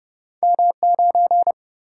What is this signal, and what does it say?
Morse code, heard as a keyed tone, about 30 wpm, 700 Hz. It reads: M 9
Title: Single-Letter Single-Number 30wpm